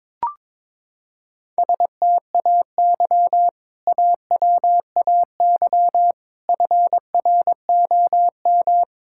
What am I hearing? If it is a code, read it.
Morse code, heard as a keyed tone, about 22 wpm, 700 Hz. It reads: STAY AWAY FROM
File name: practice / 22wpm / Sentences with 2-Word Phrases - 22wpm